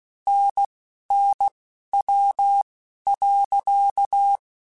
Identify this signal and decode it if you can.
Morse code, heard as a keyed tone, about 16 wpm, spaced as for 12 wpm, 800 Hz. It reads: NNW.
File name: Spaced Characters / Spaced Characters-mp3s / Weather 5 22 sp @15